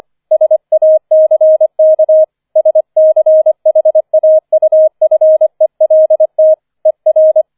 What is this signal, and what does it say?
Morse code, heard as a keyed tone, about 25 wpm, 630 Hz. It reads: SACK SCHAUFELT ER